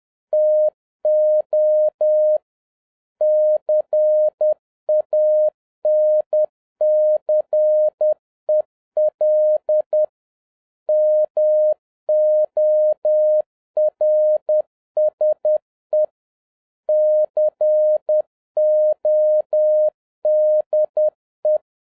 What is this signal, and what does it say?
Morse code, heard as a keyed tone, about 10 wpm, 610 Hz. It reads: TO CANCEL MORSE CODE